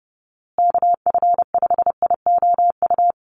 Morse code, heard as a keyed tone, about 30 wpm, 700 Hz. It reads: KF5IOU
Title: USA Call Signs Repeated 2x 30wpm